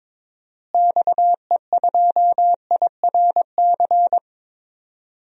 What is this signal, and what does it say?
Morse code, heard as a keyed tone, about 22 wpm, 700 Hz. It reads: XE2IRC
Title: DXCC-100 Call Signs - 6 Characters - Repeated 3x - 22wpm